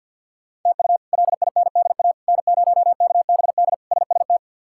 Morse code, heard as a keyed tone, about 52 wpm, 700 Hz. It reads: TU LINDA N0KBD SST
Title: K1USN SST QSO Practice - 50wpm